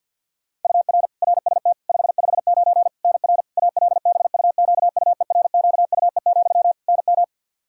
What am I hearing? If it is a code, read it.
Morse code, heard as a keyed tone, about 49 wpm, 700 Hz. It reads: UR RST 559 NR ALBUQUERQUE, NM
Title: Practice QSO - Exchange 1 - Beginning - 50wpm